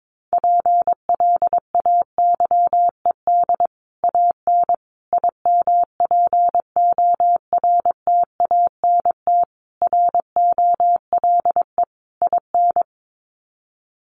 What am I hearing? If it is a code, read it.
Morse code, heard as a keyed tone, about 22 wpm, 700 Hz. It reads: PLAYED AN IMPORTANT ROLE IN
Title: Sets of 5 Words Encore2 22wpm